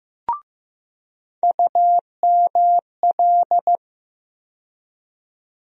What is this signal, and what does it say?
Morse code, heard as a keyed tone, about 15 wpm, 700 Hz. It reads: UML
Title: USA Call Sign Postfixes - 3 Letters 15wpm